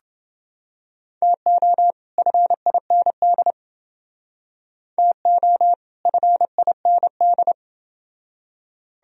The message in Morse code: TO FIND TO FIND

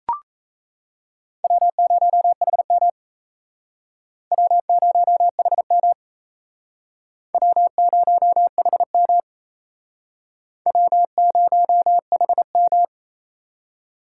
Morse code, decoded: W0HM W0HM W0HM W0HM

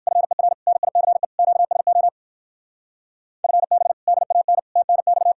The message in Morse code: FER NICE QSO FB DAN TNX

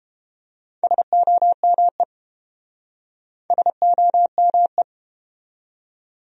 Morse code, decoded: SOME SOME

